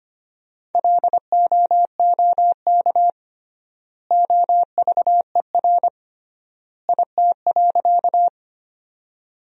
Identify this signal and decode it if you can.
Morse code, heard as a keyed tone, about 25 wpm, 700 Hz. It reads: LOOK OVER IT.
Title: 3x Word-Spacing - Sentences from Top 100 Words - 25wpm